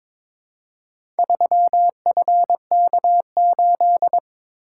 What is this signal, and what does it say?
Morse code, heard as a keyed tone, about 22 wpm, 700 Hz. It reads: 3FK8
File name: practice / 22wpm / Four Letter-Number Combinations - 22wpm